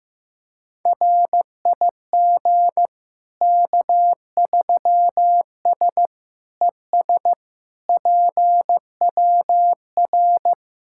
Morse code, heard as a keyed tone, about 15 wpm, 700 Hz. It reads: RIG K3S ES PWR
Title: Practice QSO - Exchange 2 - Info - 15wpm